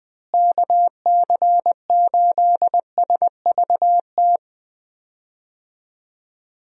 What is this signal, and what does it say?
Morse code, heard as a keyed tone, about 21 wpm, 700 Hz. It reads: KC8SVT
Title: USA Call Signs - 6 Characters - Repeated 3x - 20wpm